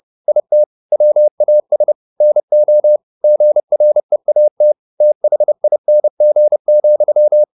Morse code, heard as a keyed tone, about 30 wpm, 600 Hz. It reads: IT WAS NO GREAT THING,